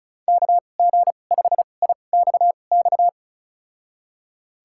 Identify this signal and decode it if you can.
Morse code, heard as a keyed tone, about 36 wpm, spaced as for 25 wpm, 700 Hz. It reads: KG5IXX
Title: USA Call Signs - 2x Character Spacing - 35wpm